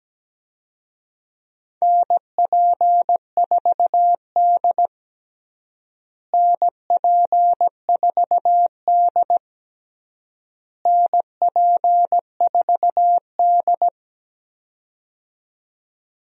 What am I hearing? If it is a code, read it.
Morse code, heard as a keyed tone, about 17 wpm, 700 Hz. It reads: NP4D NP4D NP4D
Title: USA Call Signs - 4 Characters - Repeated 3x - 17wpm